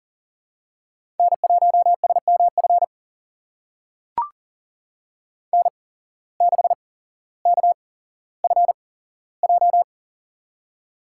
Morse code, heard as a keyed tone, about 40 wpm, spaced as for 17 wpm, 700 Hz. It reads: N1SMF N6KFJ
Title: USA Call Signs - 5 Characters - 8x Character Spacing - 40wpm